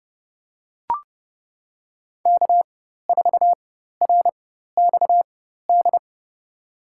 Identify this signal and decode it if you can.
Morse code, heard as a keyed tone, about 30 wpm, spaced as for 14 wpm, 700 Hz. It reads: K4RXD
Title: USA Call Signs - 5 Characters - 4x Character Spacing - 30wpm